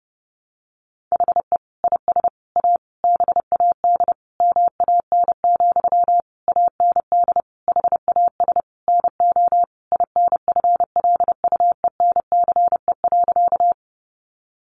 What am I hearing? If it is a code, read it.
Morse code, heard as a keyed tone, about 30 wpm, 700 Hz. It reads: HE IS A BAD MAN, AND HAS NO INFLUENCE.
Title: Sentences from Top 1100 Words 30wpm